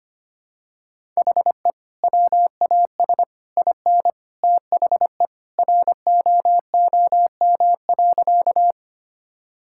Morse code, heard as a keyed tone, about 25 wpm, 700 Hz. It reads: HE WAS IN THE ROOM.
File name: practice / 25wpm / Sentences with 2-Word Phrases - 25wpm